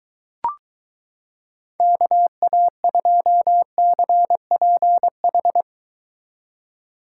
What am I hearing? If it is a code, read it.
Morse code, heard as a keyed tone, about 24 wpm, 700 Hz. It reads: KA2CPH